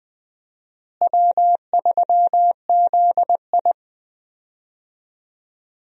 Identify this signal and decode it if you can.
Morse code, heard as a keyed tone, about 21 wpm, 700 Hz. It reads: W3ZI
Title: USA Call Signs - 4 Characters - Repeated 2x - 20wpm